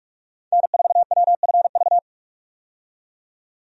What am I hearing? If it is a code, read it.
Morse code, heard as a keyed tone, about 46 wpm, 700 Hz. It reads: N4WFV
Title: USA Call Signs - 5 Characters - 45wpm